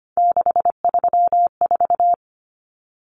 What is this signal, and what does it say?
Morse code, heard as a keyed tone, about 25 wpm, 700 Hz. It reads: 634
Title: Common Cut Numbers - 3-Digit 25wpm